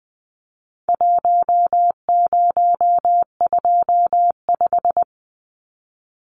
Morse code, heard as a keyed tone, about 20 wpm, 700 Hz. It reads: 1025
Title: Numbers - Contest Style - 4-Digit 20wpm